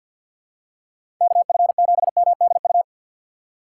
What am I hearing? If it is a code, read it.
Morse code, heard as a keyed tone, about 49 wpm, 700 Hz. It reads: KF7MDU
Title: USA Call Signs - 6 Characters - 4x Character Spacing - 50wpm